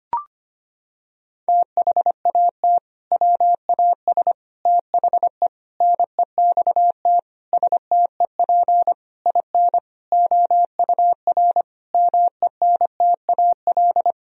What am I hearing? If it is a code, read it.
Morse code, heard as a keyed tone, about 25 wpm, 700 Hz. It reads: THAT WAS THE NEXT STEP IN OUR MENTAL
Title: Sentences from Top 1000 Words - Review Part 4 - 25wpm